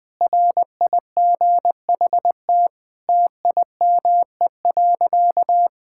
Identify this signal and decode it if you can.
Morse code, heard as a keyed tone, about 20 wpm, 700 Hz. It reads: RIGHT TIME.
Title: Sentences with 7-Word Phrases - Spoken Beginning - 20wpm